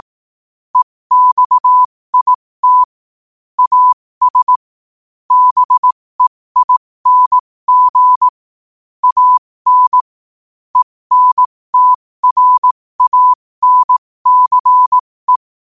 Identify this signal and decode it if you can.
Morse code, heard as a keyed tone, about 18 wpm, spaced as for 15 wpm, 1000 Hz. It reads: EXIT AS BEING AN ENTRANCE
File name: qotdfiles / 15wpm